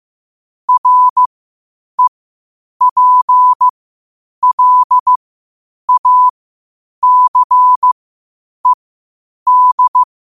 Morse code, heard as a keyed tone, about 15 wpm, spaced as for 8 wpm, 1000 Hz. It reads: REPLACED